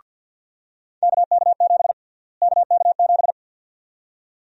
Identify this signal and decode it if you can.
Morse code, heard as a keyed tone, about 50 wpm, 700 Hz. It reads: KK7 KK7
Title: USA Call Sign Prefixes - Repeated 2x - 50wpm